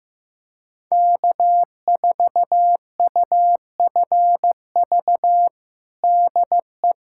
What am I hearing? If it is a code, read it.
Morse code, heard as a keyed tone, about 15 wpm, 700 Hz. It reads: K4UFV DE
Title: USA Callsign DE Callsign - 5 Characters - 15wpm